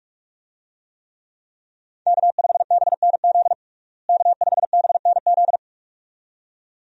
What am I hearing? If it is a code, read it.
Morse code, heard as a keyed tone, about 46 wpm, 700 Hz. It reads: K5BNZ K5BNZ
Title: USA Call Signs - 5 Characters - Repeated 2x - 45wpm